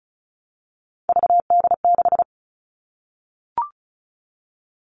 Morse code, heard as a keyed tone, about 34 wpm, 700 Hz. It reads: VD6
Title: All Cut Numbers - 3-Digit 35wpm